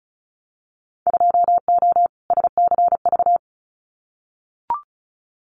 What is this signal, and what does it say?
Morse code, heard as a keyed tone, about 36 wpm, 700 Hz. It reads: 2O SCV
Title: Winter Field Day Exchanges Encore 35wpm